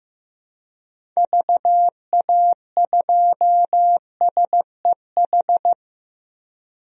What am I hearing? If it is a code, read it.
Morse code, heard as a keyed tone, about 15 wpm, 700 Hz. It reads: VA2SEH
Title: DXCC-100 Call Signs - 6 Characters - 15wpm